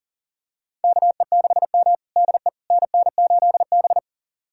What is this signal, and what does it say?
Morse code, heard as a keyed tone, about 40 wpm, 700 Hz. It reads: KE6M DE NN8B
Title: USA Callsign DE Callsign - 4 Characters - 40wpm